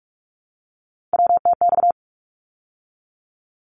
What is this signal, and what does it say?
Morse code, heard as a keyed tone, about 46 wpm, 700 Hz. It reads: WTX
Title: ARRL Sections 45wpm